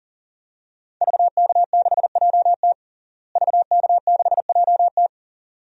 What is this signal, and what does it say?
Morse code, heard as a keyed tone, about 40 wpm, 700 Hz. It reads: VK6JT VK6JT
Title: DXCC-100 Call Signs - 5 Characters - Repeated 2x - 40wpm